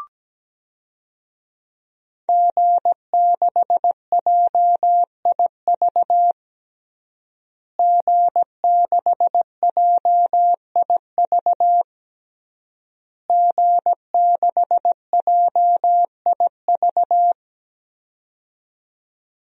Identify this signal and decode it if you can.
Morse code, heard as a keyed tone, about 17 wpm, 700 Hz. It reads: G6JIV G6JIV G6JIV
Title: DXCC-100 Call Signs - 5 Characters - Repeated 3x - 17wpm